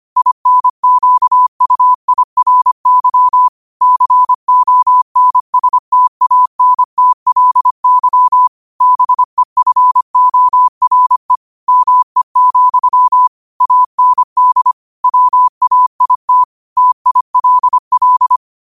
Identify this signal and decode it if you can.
Morse code, heard as a keyed tone, about 25 wpm, 1000 Hz. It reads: INQUIRY CONSTANTLY BEFORE ME, AND WAIT TILL